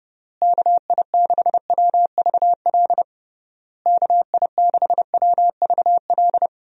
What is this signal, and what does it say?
Morse code, heard as a keyed tone, about 30 wpm, 700 Hz. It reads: KI6WVL KI6WVL
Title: USA Call Signs - 6 Characters - Repeated 3x - 30wpm